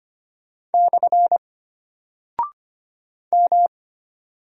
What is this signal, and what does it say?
Morse code, heard as a keyed tone, about 25 wpm, 700 Hz. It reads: / M
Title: Single Character (Letter:Number:Punctuation:Prosign) Alt 25wpm